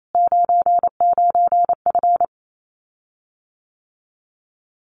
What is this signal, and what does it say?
Morse code, heard as a keyed tone, about 28 wpm, 700 Hz. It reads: 99F
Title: Common QSO Phrases Encore 28wpm